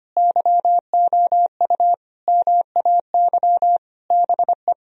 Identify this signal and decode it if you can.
Morse code, heard as a keyed tone, about 25 wpm, 700 Hz. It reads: YOU MAY BE
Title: Sentences from Top 1000 Words - Review Part 1 - 25wpm